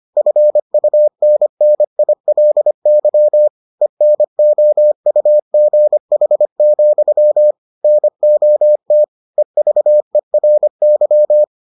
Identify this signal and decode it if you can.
Morse code, heard as a keyed tone, about 25 wpm, 600 Hz. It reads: FUNNILY ENOUGH, NOT EVERY